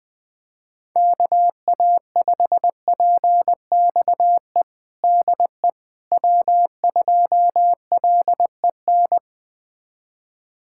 Morse code, heard as a keyed tone, about 20 wpm, 700 Hz. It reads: KA5PXE DE W2LEN
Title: USA Callsign DE Callsign 20wpm